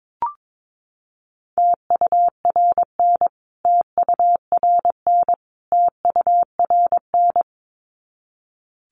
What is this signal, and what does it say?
Morse code, heard as a keyed tone, about 22 wpm, 700 Hz. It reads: TURN TURN TURN
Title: Top 300 Words Repeated 22wpm